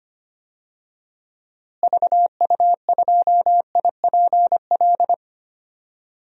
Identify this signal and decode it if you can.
Morse code, heard as a keyed tone, about 25 wpm, 700 Hz. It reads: VU2IPL
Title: DXCC-100 Call Signs - 6 Characters - Repeated 3x - 25wpm